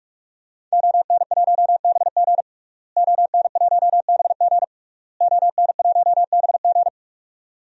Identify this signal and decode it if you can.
Morse code, heard as a keyed tone, about 44 wpm, 700 Hz. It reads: ON1BG ON1BG ON1BG
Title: DXCC-100 Call Signs - 5 Characters - Repeated 3x - 45wpm